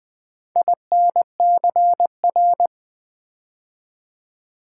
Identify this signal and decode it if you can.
Morse code, heard as a keyed tone, about 20 wpm, 700 Hz. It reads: INCR
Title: Build A Word - 4 Letters At A Time - 12-Letter Words - 20wpm